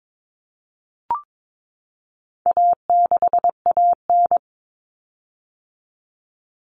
Morse code, heard as a keyed tone, about 22 wpm, 700 Hz. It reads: A6AN
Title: Common Cut Numbers - 4-Digit 22wpm